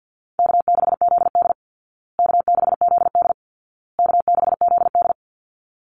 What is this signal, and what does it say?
Morse code, heard as a keyed tone, about 50 wpm, 700 Hz. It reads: K6ZD K6ZD K6ZD
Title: Easy Call Signs Repeated 3x - 50wpm